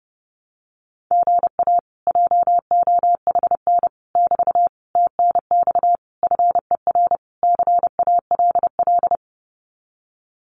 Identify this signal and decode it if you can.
Morse code, heard as a keyed tone, about 30 wpm, 700 Hz. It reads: GA JOHN <BT> TNX FER CALL
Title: Common QSO Phrases Encore 30wpm